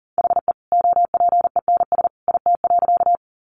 Morse code, heard as a keyed tone, about 40 wpm, 700 Hz. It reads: HE OPENS IT.
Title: Additional Easy Phrases 40wpm